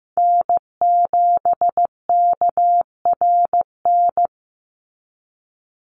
Morse code, heard as a keyed tone, about 15 wpm, 700 Hz. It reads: N7KRN
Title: NA SOTA Chaser Callsigns 15wpm